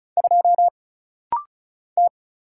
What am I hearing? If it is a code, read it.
Morse code, heard as a keyed tone, about 36 wpm, 700 Hz. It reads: 2 T
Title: Single Letter-Number Alt - Rapid-Fire 35wpm